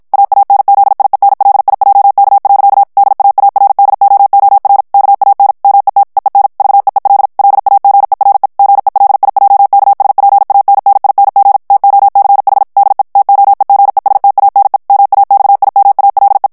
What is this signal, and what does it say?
Morse code, heard as a keyed tone, about 56 wpm, 800 Hz. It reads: AANZIENLIJK. DAARDOOR KAN MET EEN VEEL LAGERE GELIJKSPANNING TOCH DE TOEGESTANE MAXIMALE
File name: HOLLANDAIS texte clair 2 vitesse 50 mots minute